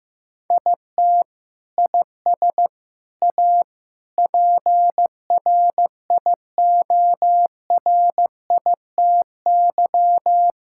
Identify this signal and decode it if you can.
Morse code, heard as a keyed tone, about 15 wpm, 700 Hz. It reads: IT IS A PRIORITY
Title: Sentences from Top 1600 Words 15wpm